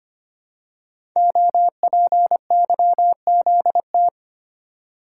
Morse code, heard as a keyed tone, about 25 wpm, 700 Hz. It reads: OPYZT
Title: Code Groups - 5 Letters - 25wpm